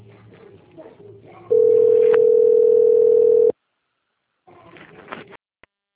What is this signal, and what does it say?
Ringback tone (North America), 440 + 480 Hz.